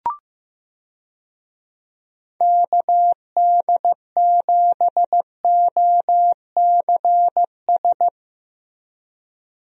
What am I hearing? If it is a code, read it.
Morse code, heard as a keyed tone, about 15 wpm, 700 Hz. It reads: KD7OCS
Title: USA Call Signs - 6 Characters - Repeated 3x - 15wpm